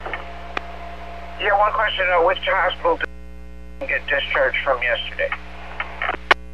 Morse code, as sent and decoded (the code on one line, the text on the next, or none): none